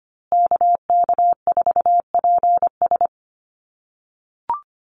Morse code, heard as a keyed tone, about 25 wpm, 700 Hz. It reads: KK4PS
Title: USA Call Signs Repeated 3x 25wpm